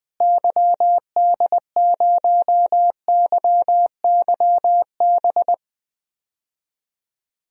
Morse code, heard as a keyed tone, about 20 wpm, 700 Hz. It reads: YD0YYB
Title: DXCC-100 Call Signs - 6 Characters - Repeated 3x - 20wpm